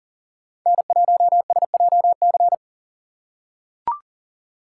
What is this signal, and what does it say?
Morse code, heard as a keyed tone, about 40 wpm, 700 Hz. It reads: N1SJC
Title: USA Call Signs - 5 Characters - 3x Character Spacing - 40wpm